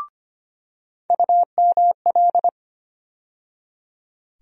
Morse code, heard as a keyed tone, about 25 wpm, 700 Hz. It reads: UML
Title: USA Call Sign Postfixes - 3 Letters 25wpm